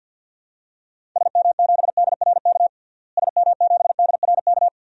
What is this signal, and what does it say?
Morse code, heard as a keyed tone, about 45 wpm, 700 Hz. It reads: SM7DRK SM7DRK
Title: DXCC-100 Call Signs - 6 Characters - Repeated 2x - 50wpm